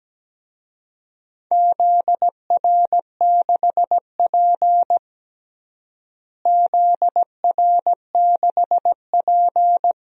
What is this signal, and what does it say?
Morse code, heard as a keyed tone, about 17 wpm, 700 Hz. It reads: ZR6P ZR6P
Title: DXCC-100 Call Signs - 4 Characters - Repeated 3x - 17wpm